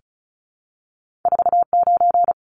Morse code, heard as a keyed tone, about 36 wpm, 700 Hz. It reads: 49
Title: Numbers - 2-Digit 35wpm